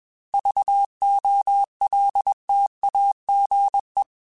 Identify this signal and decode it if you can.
Morse code, heard as a keyed tone, about 21 wpm, 800 Hz. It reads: VOLTAGE
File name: Technical Subjects / Technical Subjects-mp3s / XC @20